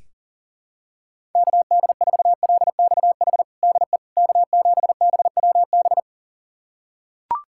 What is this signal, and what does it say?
Morse code, heard as a keyed tone, about 40 wpm, 700 Hz. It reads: KD4LXH DE K7BWB
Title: USA Callsign DE Callsign 40wpm